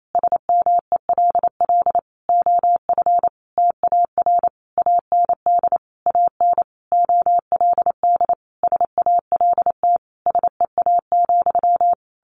Morse code, heard as a keyed tone, about 28 wpm, 700 Hz. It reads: SMELL OF TAR AND AN OLD SALT SEA,
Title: A Sound Of Thunder 28wpm